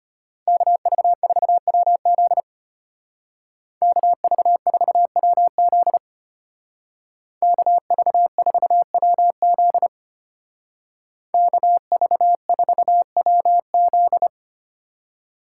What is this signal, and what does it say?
Morse code, heard as a keyed tone, about 32 wpm, 700 Hz. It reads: KV4WZ KV4WZ KV4WZ KV4WZ